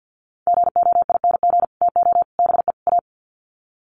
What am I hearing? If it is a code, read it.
Morse code, heard as a keyed tone, about 50 wpm, 700 Hz. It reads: GOING TO BE A
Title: Sets of 4 Words 50wpm